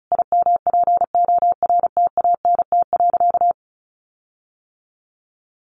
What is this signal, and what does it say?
Morse code, heard as a keyed tone, about 34 wpm, 700 Hz. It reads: IMPORTANT.
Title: Sentences from Top 1300 Words 35wpm